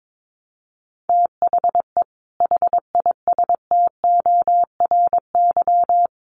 Morse code, heard as a keyed tone, about 22 wpm, 700 Hz. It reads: THE HISTORY
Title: Sets of 3 Words 22wpm